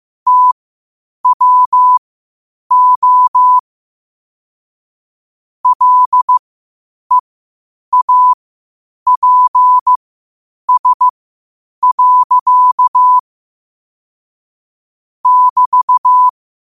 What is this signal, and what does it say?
Morse code, heard as a keyed tone, about 15 wpm, spaced as for 8 wpm, 1000 Hz. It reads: TWO LEAPS. <BT>
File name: qotdfiles / 8wpm